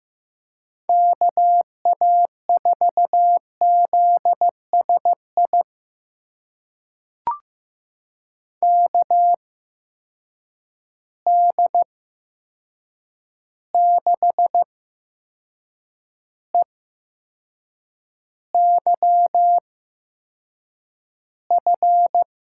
Morse code, heard as a keyed tone, about 15 wpm, spaced as for 6 wpm, 700 Hz. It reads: KA4ZSI KD6EYF
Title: USA Call Signs - 6 Characters - 8x Character Spacing - 15wpm